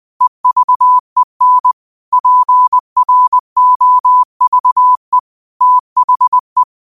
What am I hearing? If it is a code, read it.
Morse code, heard as a keyed tone, about 20 wpm, 1000 Hz. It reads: EVEN PROVE THE
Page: New quotes every day in morse code at 20 Words per minute.